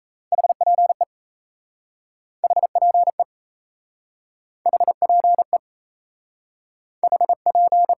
Morse code, heard as a keyed tone, about 37 wpm, 700 Hz. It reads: HPE HPE HPE HP